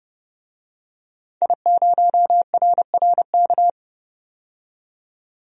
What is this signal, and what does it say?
Morse code, heard as a keyed tone, about 30 wpm, 690 Hz. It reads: I0RRK
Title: DXCC-100 Call Signs - 5 Characters - 30wpm